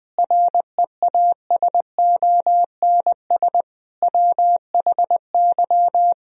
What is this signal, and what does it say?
Morse code, heard as a keyed tone, about 21 wpm, 700 Hz. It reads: REASONS WHY